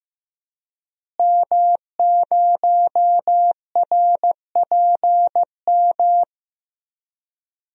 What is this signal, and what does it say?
Morse code, heard as a keyed tone, about 15 wpm, 700 Hz. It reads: M0RPM
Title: DXCC-100 Call Signs - 5 Characters - Repeated 3x - 15wpm